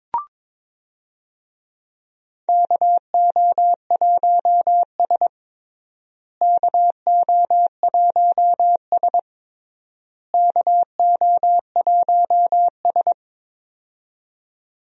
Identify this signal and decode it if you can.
Morse code, heard as a keyed tone, about 22 wpm, 700 Hz. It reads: KO1S KO1S KO1S
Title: USA Call Signs - 4 Characters - Repeated 3x - 22wpm